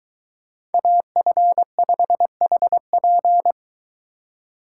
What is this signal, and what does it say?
Morse code, heard as a keyed tone, about 22 wpm, 700 Hz. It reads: AF5HP